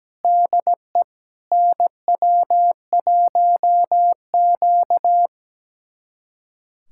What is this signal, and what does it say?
Morse code, heard as a keyed tone, about 17 wpm, 700 Hz. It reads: DE NW1Q
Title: USA Callsign DE Callsign - 4 Characters - 17wpm